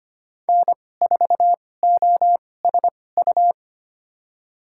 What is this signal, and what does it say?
Morse code, heard as a keyed tone, about 25 wpm, spaced as for 18 wpm, 700 Hz. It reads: N4OSU
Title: USA Call Signs - 5 Characters - 2x Character Spacing - 25wpm